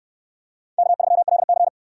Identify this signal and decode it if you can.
Morse code, heard as a keyed tone, about 68 wpm, 700 Hz. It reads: K3LC